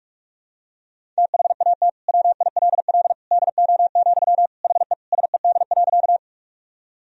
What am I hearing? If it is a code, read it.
Morse code, heard as a keyed tone, about 47 wpm, 700 Hz. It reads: THAT WILL DO, HE SED.
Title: Sentences from Top 100 Words and Abbreviations 45wpm